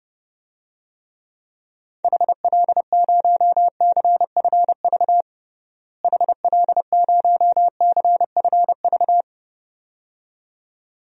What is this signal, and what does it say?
Morse code, heard as a keyed tone, about 31 wpm, 700 Hz. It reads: HL0CFV HL0CFV
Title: DXCC-100 Call Signs - 6 Characters - Repeated 2x - 30wpm